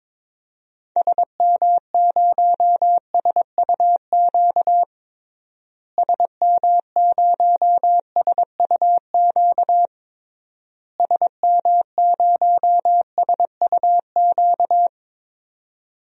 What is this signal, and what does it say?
Morse code, heard as a keyed tone, about 22 wpm, 700 Hz. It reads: SM0SUQ SM0SUQ SM0SUQ
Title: DXCC-100 Call Signs - 6 Characters - Repeated 3x - 22wpm